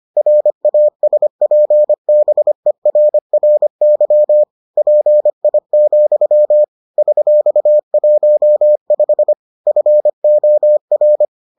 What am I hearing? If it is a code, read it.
Morse code, heard as a keyed tone, about 25 wpm, 600 Hz. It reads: RASPBERRY PI, $15 FOR